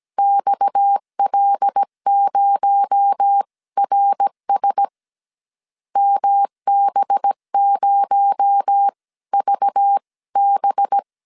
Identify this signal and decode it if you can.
Morse code, heard as a keyed tone, about 17 wpm, spaced as for 15 wpm, 800 Hz. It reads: XL0RS MB0VB
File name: CW / DET / 1 6 1500C S A 16wpm 2635